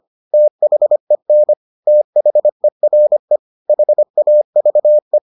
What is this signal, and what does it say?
Morse code, heard as a keyed tone, about 25 wpm, 600 Hz. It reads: THEN THERE HAVE